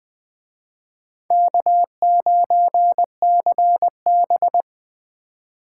Morse code, heard as a keyed tone, about 20 wpm, 700 Hz. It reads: K9CB
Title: Speed Racing - USA Call Signs 17wpm